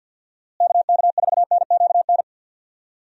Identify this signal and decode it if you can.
Morse code, heard as a keyed tone, about 50 wpm, 700 Hz. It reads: KK4NQN
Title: USA Call Signs - 6 Characters - Repeated 3x - 50wpm